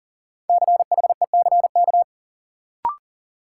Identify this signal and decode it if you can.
Morse code, heard as a keyed tone, about 40 wpm, 700 Hz. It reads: CHECK
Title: Sentences from Top 1000 Words - Review Part 3 - New Word Drill - 40wpm